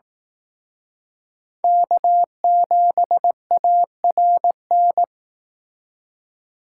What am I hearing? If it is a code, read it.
Morse code, heard as a keyed tone, about 18 wpm, 700 Hz. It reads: K7ARN
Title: Speed Racing - USA 1x3 Call Signs 15wpm